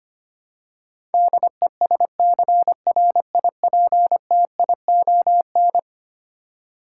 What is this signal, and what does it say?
Morse code, heard as a keyed tone, about 25 wpm, 700 Hz. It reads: DESCRIPTION
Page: morse-code-ninja-practice-25wpm